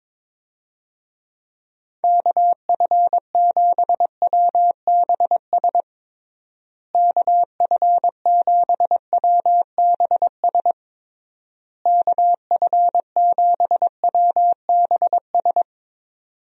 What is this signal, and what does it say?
Morse code, heard as a keyed tone, about 22 wpm, 700 Hz. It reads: KF7WBS KF7WBS KF7WBS
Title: USA Call Signs - 6 Characters - Repeated 3x - 22wpm